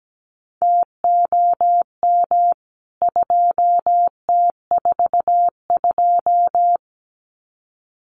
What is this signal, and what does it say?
Morse code, heard as a keyed tone, about 17 wpm, 700 Hz. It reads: TOM 2T42
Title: CWT Exchange 17wpm